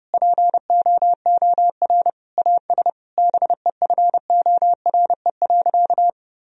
Morse code, heard as a keyed tone, about 32 wpm, 700 Hz. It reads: POOR AS BEFORE.
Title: Sentences from Top 1000 Words - Review Part 2 - 30wpm